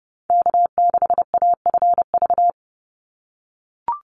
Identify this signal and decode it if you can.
Morse code, heard as a keyed tone, about 30 wpm, 700 Hz. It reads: K6AFV
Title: USA Call Signs 30wpm